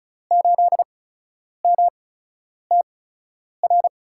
Morse code, heard as a keyed tone, about 36 wpm, spaced as for 10 wpm, 700 Hz. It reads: 8MTR
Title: USA Call Signs - 8x Character Spacing - 35wpm